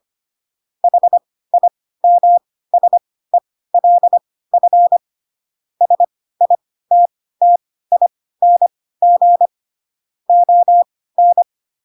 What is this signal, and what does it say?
Morse code, heard as a keyed tone, about 25 wpm, spaced as for 16 wpm, 700 Hz. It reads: HIMSELF SITTING ON
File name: Pinocchio / Pinocchio ch1 10 25 ik1hge